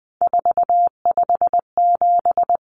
Morse code, heard as a keyed tone, about 20 wpm, 700 Hz. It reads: 457
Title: Numbers - 3-Digit 20wpm